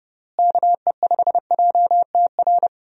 Morse code, heard as a keyed tone, about 30 wpm, 700 Hz. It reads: KE5JTR
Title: USA Call Signs - 6 Characters - Repeated 3x - 30wpm